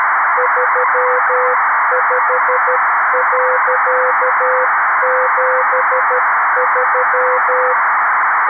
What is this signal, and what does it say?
Morse code, heard as a keyed tone, about 13 wpm, 510 Hz, with noise. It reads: 35.73
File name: arissat / recordings / ARISSat 11Sept2011 1359utc 2mins 14frames